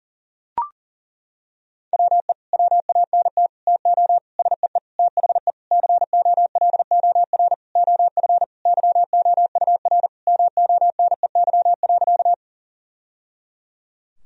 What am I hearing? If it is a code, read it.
Morse code, heard as a keyed tone, about 37 wpm, 700 Hz. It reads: WE WANT TO SEE THE COLOR OF YOUR MONEY.
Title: Sentences from Top 1000 Words - Review Part 2 - 40wpm